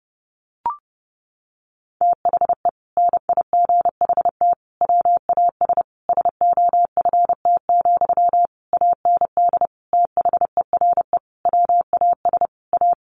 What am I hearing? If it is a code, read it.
Morse code, heard as a keyed tone, about 30 wpm, 700 Hz. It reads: THE NIGHT WAS SOFT, AND THERE WAS A
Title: Sentences from Top 1500 Words 30wpm